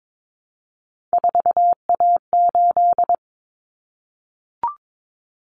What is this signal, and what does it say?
Morse code, heard as a keyed tone, about 22 wpm, 700 Hz. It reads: 4A8
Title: Common Cut Numbers - 3-Digit 22wpm